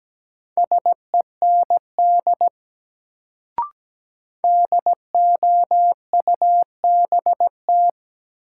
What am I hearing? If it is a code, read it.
Morse code, heard as a keyed tone, about 17 wpm, 700 Hz. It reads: SEND DOUBT
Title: Sentences from Top 1500 Words - New Word Drill Alt - 17wpm